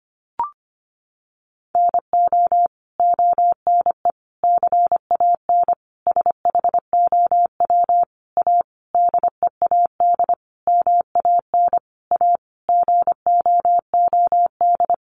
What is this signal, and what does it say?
Morse code, heard as a keyed tone, about 25 wpm, 700 Hz. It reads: NO ONE CAN SHOW A DEAD MAN A GOOD
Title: Sentences from Top 1500 Words 25wpm